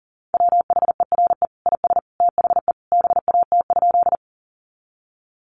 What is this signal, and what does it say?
Morse code, heard as a keyed tone, about 40 wpm, 700 Hz. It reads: WHERE IS THE BAT?
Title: Sentences from Top 1500 Words 40wpm